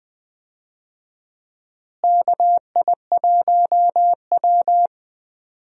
Morse code, heard as a keyed tone, about 19 wpm, 700 Hz. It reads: KI1W
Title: USA Call Signs - 4 Characters - Repeated 3x - 20wpm